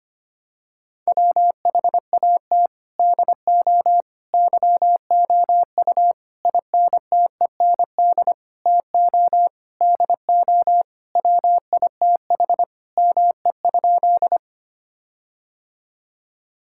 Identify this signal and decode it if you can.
Morse code, heard as a keyed tone, about 25 wpm, 700 Hz. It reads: WHAT DO YOU INTEND TO DO WITH ME?
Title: Sentences from Top 1600 Words 25wpm